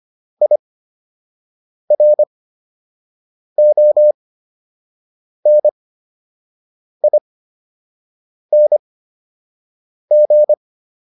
Morse code, heard as a keyed tone, about 25 wpm, spaced as for 6 wpm, 600 Hz. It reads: IRONING